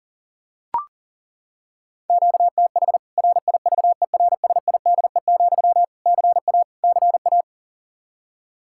Morse code, heard as a keyed tone, about 40 wpm, 700 Hz. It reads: QTH RIVERSIDE, CA CA
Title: K1USN SST QSO Practice 40wpm